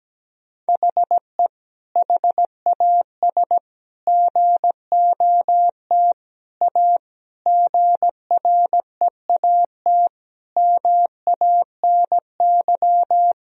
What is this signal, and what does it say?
Morse code, heard as a keyed tone, about 17 wpm, 700 Hz. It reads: HE HAS GOT A GREAT MANY
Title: Sentences from Top 1000 Words - Review Part 1 - 17wpm